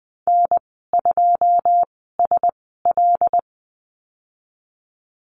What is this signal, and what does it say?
Morse code, heard as a keyed tone, about 20 wpm, spaced as for 15 wpm, 700 Hz. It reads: N2SL
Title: 128 Lesson 21g - Number 6 - Call Signs 20wpm